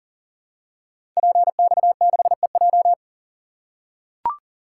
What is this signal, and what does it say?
Morse code, heard as a keyed tone, about 42 wpm, 690 Hz. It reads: PX6EJ
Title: Code Groups - 5 Letter:Numbers - 40wpm